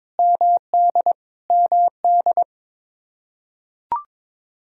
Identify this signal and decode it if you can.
Morse code, heard as a keyed tone, about 22 wpm, 700 Hz. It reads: MD MD
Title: Practice QSO - Exchange 1 - Beginning - 22wpm